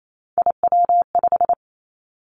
Morse code, heard as a keyed tone, about 28 wpm, 700 Hz. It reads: IW5
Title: Three Challenging Letter-Number Combinations 28wpm